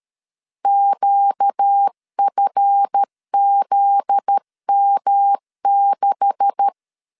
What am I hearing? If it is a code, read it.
Morse code, heard as a keyed tone, about 13 wpm, 800 Hz. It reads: QFZM6
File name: LEZIONI / 9 286 12wpm 2305